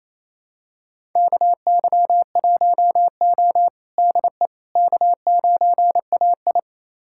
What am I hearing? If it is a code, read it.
Morse code, heard as a keyed tone, about 28 wpm, 700 Hz. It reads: KY1O DE K9AI
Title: USA Callsign DE Callsign - 4 Characters - 28wpm